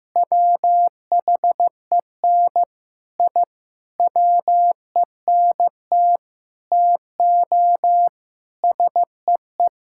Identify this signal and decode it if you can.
Morse code, heard as a keyed tone, about 15 wpm, 700 Hz. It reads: WHEN I WENT TO SEE
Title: Sentences from Top 1000 Words - Review Part 5 - 15wpm